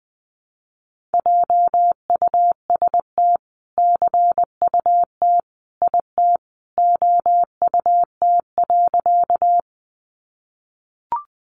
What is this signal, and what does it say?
Morse code, heard as a keyed tone, about 20 wpm, 700 Hz. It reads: JUST CUT IT OUT.
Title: Sentences from Top 300 Words 20wpm